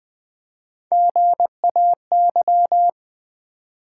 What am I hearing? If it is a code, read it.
Morse code, heard as a keyed tone, about 20 wpm, 700 Hz. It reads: GAY
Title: Build A Word - 1 Letter At A Time - 3-Letter Words - 20wpm